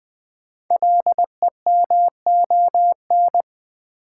Morse code, heard as a keyed tone, about 20 wpm, 700 Hz. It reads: LEMON
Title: Build A Word - 1 Letter At A Time - 5-Letter Words - 20wpm